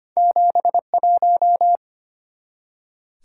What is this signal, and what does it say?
Morse code, heard as a keyed tone, about 25 wpm, 700 Hz. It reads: 71
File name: Two Letter-Number Combinations - 25wpm